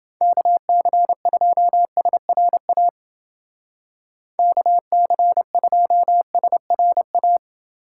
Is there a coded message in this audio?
Yes: Morse code from 0.2 to 7.4 s, heard as a keyed tone, about 29 wpm, 700 Hz. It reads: KC2SRA KC2SRA